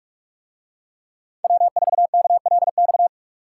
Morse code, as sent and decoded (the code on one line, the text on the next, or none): .-- ....- -.- .-.. -..-
W4KLX